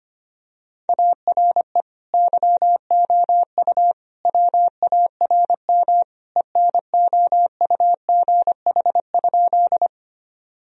Morse code, heard as a keyed tone, about 25 wpm, 700 Hz. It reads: ARE YOU WARM ENOUGH?
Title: Sentences from Top 1000 Words - Review Part 2 - 25wpm